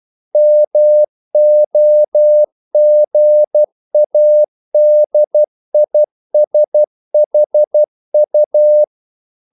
Morse code, heard as a keyed tone, about 12 wpm, 600 Hz. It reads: MOGADISHU
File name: CW / Practice / NationalCapitals12